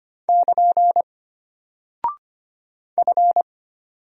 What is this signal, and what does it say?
Morse code, heard as a keyed tone, about 25 wpm, 700 Hz. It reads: <KN> F
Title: Single Character (Letter:Number:Punctuation:Prosign) Alt 25wpm